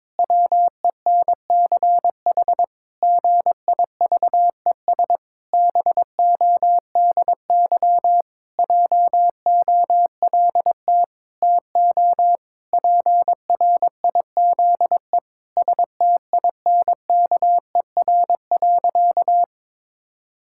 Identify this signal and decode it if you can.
Morse code, heard as a keyed tone, about 22 wpm, 700 Hz. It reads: WENCH GIVES BODY JOLT TO PRIZE STINKER.
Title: Pangram Challenge v2 - 22wpm